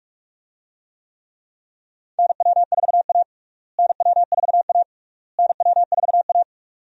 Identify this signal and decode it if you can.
Morse code, heard as a keyed tone, about 46 wpm, 700 Hz. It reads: NW4A NW4A NW4A
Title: USA Call Signs - 4 Characters - Repeated 3x - 45wpm